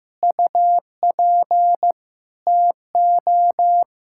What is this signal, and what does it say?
Morse code, heard as a keyed tone, about 15 wpm, 700 Hz. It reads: UP TO
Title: Pangram Challenge v2 - 15wpm